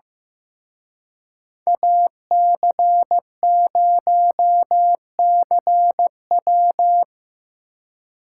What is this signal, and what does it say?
Morse code, heard as a keyed tone, about 17 wpm, 700 Hz. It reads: AC0CW
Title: Speed Racing - USA 2x2 Call Signs 15wpm